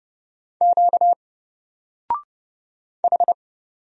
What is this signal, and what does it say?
Morse code, heard as a keyed tone, about 30 wpm, 700 Hz. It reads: Q H
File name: Single-Letter-Number-30wpm